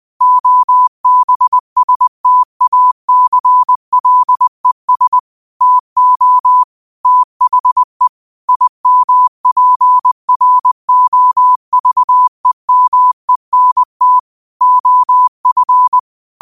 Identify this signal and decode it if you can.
Morse code, heard as a keyed tone, about 20 wpm, 1000 Hz. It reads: OBSTACLES TO THE IMPROVEMENT OF
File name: qotdfiles / 20wpm